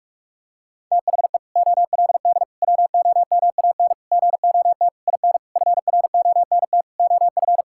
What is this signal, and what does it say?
Morse code, heard as a keyed tone, about 42 wpm, 700 Hz. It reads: THE OLD WOMAN GOT IN FRONT OF
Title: Sentences from Top 1600 Words 45wpm